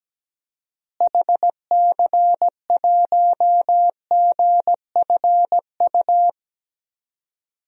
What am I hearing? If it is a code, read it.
Morse code, heard as a keyed tone, about 17 wpm, 700 Hz. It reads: HC1GFU
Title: DXCC-100 Call Signs - 6 Characters - Repeated 3x - 17wpm